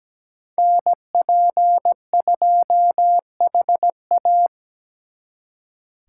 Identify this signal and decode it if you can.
Morse code, heard as a keyed tone, about 20 wpm, 700 Hz. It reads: NP2HA
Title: Speed Racing - USA 2x2 Call Signs 17wpm